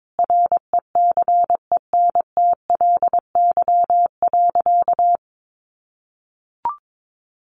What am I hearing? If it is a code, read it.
Morse code, heard as a keyed tone, about 22 wpm, 700 Hz. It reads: RECENTLY.
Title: Sentences from Top 1200 Words 22wpm